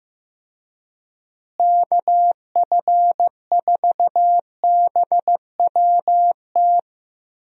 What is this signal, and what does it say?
Morse code, heard as a keyed tone, about 15 wpm, 700 Hz. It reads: KF4BWT
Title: USA Call Signs - 6 Characters - Repeated 3x - 15wpm